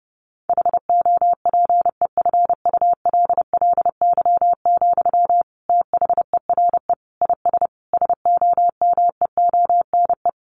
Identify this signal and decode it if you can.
Morse code, heard as a keyed tone, about 30 wpm, 700 Hz. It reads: HOPEFULLY, THERE IS SOMEONE
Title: Sentences from Top 1500 Words 30wpm